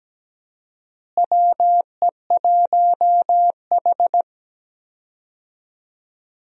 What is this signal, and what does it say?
Morse code, heard as a keyed tone, about 17 wpm, 700 Hz. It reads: WE1H
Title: USA Call Signs - 4 Characters - Repeated 3x - 17wpm